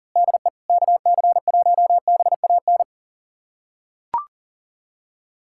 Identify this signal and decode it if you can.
Morse code, heard as a keyed tone, about 40 wpm, 700 Hz. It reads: DE KC1BAN
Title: USA Callsign DE Callsign - 6 Characters - 40wpm